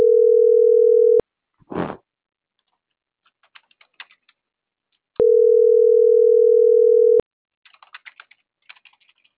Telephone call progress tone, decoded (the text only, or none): ringback tone